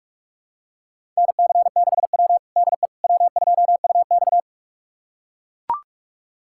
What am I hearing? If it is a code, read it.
Morse code, heard as a keyed tone, about 41 wpm, 700 Hz. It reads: NC6W DE W2UX
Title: USA Callsign DE Callsign - 4 Characters - 45wpm